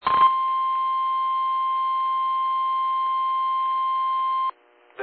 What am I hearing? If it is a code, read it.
Emergency Alert System (SAME) tones (the 1050 Hz attention tone). It is heard from 0.1 to 4.6 s.
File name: eas / KIMB / KIMB